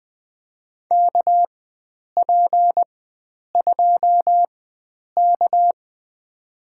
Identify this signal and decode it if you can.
Morse code, heard as a keyed tone, about 21 wpm, spaced as for 9 wpm, 700 Hz. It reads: KP2K
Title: USA Call Signs - 4 Characters - 4x Character Spacing - 20wpm